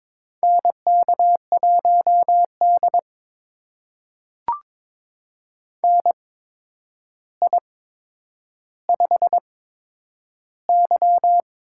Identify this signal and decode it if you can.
Morse code, heard as a keyed tone, about 22 wpm, spaced as for 9 wpm, 700 Hz. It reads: NK1D NI5Y
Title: USA Call Signs - 4 Characters - 8x Character Spacing - 22wpm